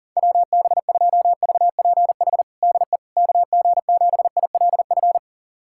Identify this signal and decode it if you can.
Morse code, heard as a keyed tone, about 40 wpm, 700 Hz. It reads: WB2VPH DE KG7ILF
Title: USA Callsign DE Callsign 40wpm